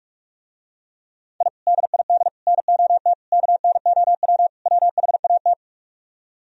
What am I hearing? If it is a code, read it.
Morse code, heard as a keyed tone, about 44 wpm, 700 Hz. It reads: I DID NOT KNOW WHAT